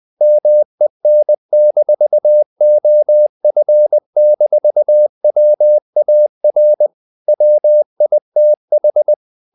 Morse code, heard as a keyed tone, about 20 wpm, 600 Hz. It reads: MEN-OF-WAR WITH
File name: the old man and the sea / the old man and the sea part 1 20wpm